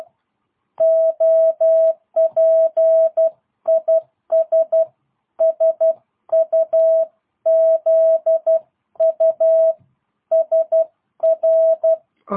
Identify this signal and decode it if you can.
Morse code, heard as a keyed tone, about 12 wpm, 660 Hz. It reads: OPIS SUZU SR
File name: jh8cbh / 2022cw